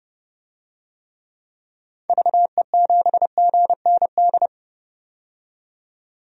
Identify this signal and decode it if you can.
Morse code, heard as a keyed tone, about 32 wpm, 700 Hz. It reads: VE7GND
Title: DXCC-100 Call Signs - 6 Characters - 30wpm